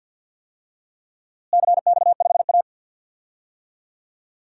Morse code, heard as a keyed tone, about 47 wpm, 700 Hz. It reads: CX5A
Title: DXCC-100 Call Signs - 4 Characters - Repeated 3x - 50wpm